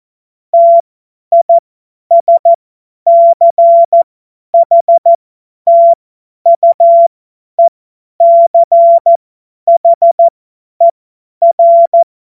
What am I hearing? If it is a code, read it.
Morse code, heard as a keyed tone, about 14 wpm, spaced as for 10 wpm, 680 Hz. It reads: TISCHTUECHER
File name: Morsekurs / Witze / Witz 25. Mai 2021, 50 F70